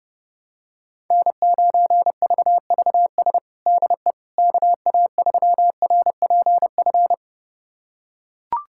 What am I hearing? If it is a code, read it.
Morse code, heard as a keyed tone, about 32 wpm, 700 Hz. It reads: N9VVS DE KA3RPF
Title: USA Callsign DE Callsign 30wpm